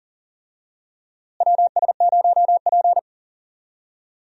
Morse code, heard as a keyed tone, about 40 wpm, 700 Hz. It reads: WS0P
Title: USA Call Signs - 4 Characters - Repeated 3x - 40wpm